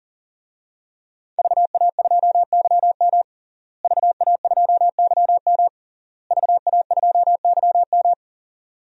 Morse code, heard as a keyed tone, about 40 wpm, 700 Hz. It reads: VA2YM VA2YM VA2YM
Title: DXCC-100 Call Signs - 5 Characters - Repeated 3x - 40wpm